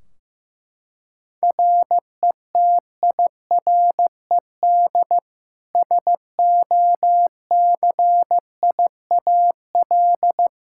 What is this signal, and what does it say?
Morse code, heard as a keyed tone, about 15 wpm, 700 Hz. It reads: RETIRED SOCIAL
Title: Practice QSO - Exchange 3 - More Info - 15wpm